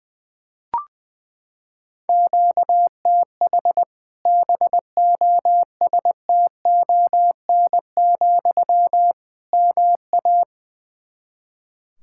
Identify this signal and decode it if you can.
Morse code, heard as a keyed tone, about 20 wpm, 700 Hz. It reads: QTH BOSTON, MA
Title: Practice QSO - Exchange 1 - Beginning - 20wpm